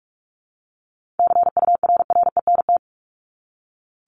Morse code, heard as a keyed tone, about 44 wpm, 700 Hz. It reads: CURRENT
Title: LY Suffix - Top 100 Words with Base Words 45wpm